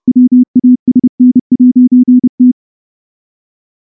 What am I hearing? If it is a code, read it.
Morse code, heard as a keyed tone, about 28 wpm, 250 Hz. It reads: WASN'T
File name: morse-training-30WPM